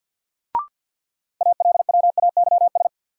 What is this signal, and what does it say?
Morse code, heard as a keyed tone, about 48 wpm, 700 Hz. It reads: ALWAYS
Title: Sentences from Top 300 Words - New Word Drill - 50wpm